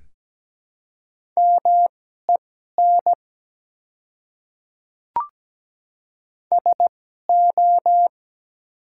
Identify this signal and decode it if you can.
Morse code, heard as a keyed tone, about 17 wpm, spaced as for 12 wpm, 700 Hz. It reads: MEN SO
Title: 2x Character Spacing - Top 200 Words - 17wpm